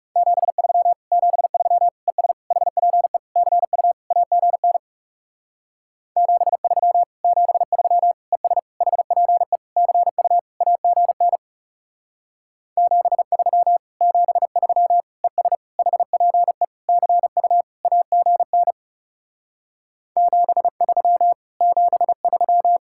Morse code, heard as a keyed tone, about 37 wpm, 700 Hz. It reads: 73 73 ES HPE CU AGN 73 73 ES HPE CU AGN 73 73 ES HPE CU AGN 73 73
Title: Speed-Racing Common QSO Phrases - 30wpm